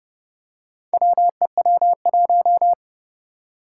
Morse code, heard as a keyed tone, about 30 wpm, 700 Hz. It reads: WEW1
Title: Four Challenging Letter-Number Combinations 30wpm